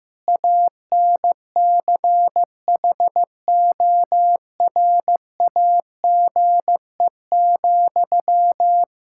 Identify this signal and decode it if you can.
Morse code, heard as a keyed tone, about 15 wpm, 700 Hz. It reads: ANCHORAGE,
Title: Practice QSO - Exchange 1 - Beginning - 15wpm